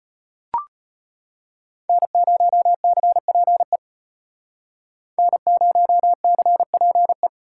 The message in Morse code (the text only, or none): N0CPE N0CPE